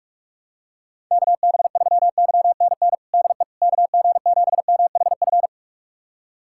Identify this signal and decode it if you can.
Morse code, heard as a keyed tone, about 46 wpm, 700 Hz. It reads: KB3YNN DE KG7MHF
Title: USA Callsign DE Callsign - 6 Characters - 45wpm